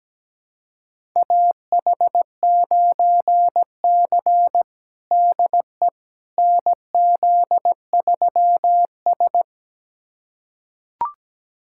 Morse code, heard as a keyed tone, about 17 wpm, 700 Hz. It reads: AH9C DE NZ3S
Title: USA Callsign DE Callsign - 4 Characters - 17wpm